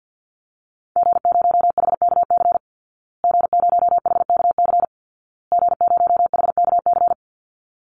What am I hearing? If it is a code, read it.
Morse code, heard as a keyed tone, about 50 wpm, 700 Hz. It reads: G0HKC G0HKC G0HKC
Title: Call Signs Repeated 2x - 50wpm